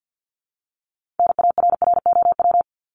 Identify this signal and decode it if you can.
Morse code, heard as a keyed tone, about 50 wpm, 700 Hz. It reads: NARROW
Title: Top 1501-1750 Words 50wpm